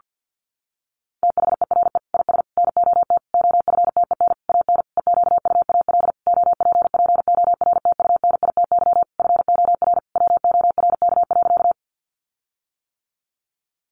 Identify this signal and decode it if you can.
Morse code, heard as a keyed tone, about 50 wpm, 700 Hz. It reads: THERE IS NOT OFTEN AN EQUAL OPPORTUNITY FOR WORK.
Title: Sentences from Top 1400 Words 50wpm